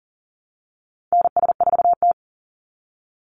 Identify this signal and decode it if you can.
Morse code, heard as a keyed tone, about 40 wpm, 700 Hz. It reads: NS4T
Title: Call Signs Repeated 3x - 40wpm